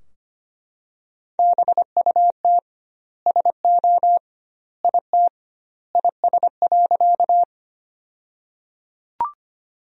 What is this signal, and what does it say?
Morse code, heard as a keyed tone, about 25 wpm, 700 Hz. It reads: BUT SO IT IS.
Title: 2x Word-Spacing - Sentences from Top 100 Words - 25wpm